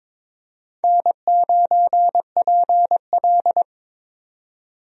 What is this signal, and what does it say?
Morse code, heard as a keyed tone, about 22 wpm, 700 Hz. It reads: N9PL
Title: USA Call Signs - 4 Characters - 22wpm